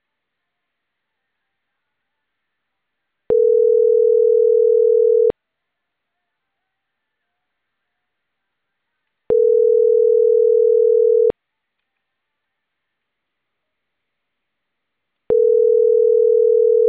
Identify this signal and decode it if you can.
Ringback tone (North America), 440 + 480 Hz.